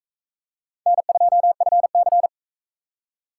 Morse code, heard as a keyed tone, about 38 wpm, 700 Hz. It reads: N2FC